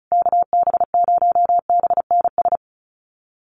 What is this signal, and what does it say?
Morse code, heard as a keyed tone, about 34 wpm, 700 Hz. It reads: KB0BNS
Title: USA Call Signs Repeated 2x 35wpm